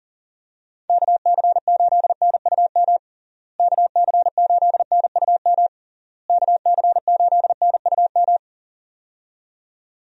Morse code, heard as a keyed tone, about 40 wpm, 700 Hz. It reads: KC8NUM KC8NUM KC8NUM
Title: USA Call Signs - 6 Characters - Repeated 3x - 40wpm